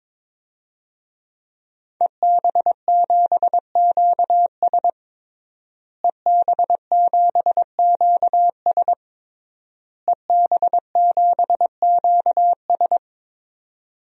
Morse code, heard as a keyed tone, about 22 wpm, 700 Hz. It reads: EB7QS EB7QS EB7QS
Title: DXCC-100 Call Signs - 5 Characters - Repeated 3x - 22wpm